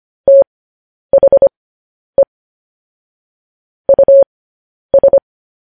Morse code, heard as a keyed tone, about 25 wpm, spaced as for 10 wpm, 550 Hz. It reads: THE US
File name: podcast / News headlines-1x-25x10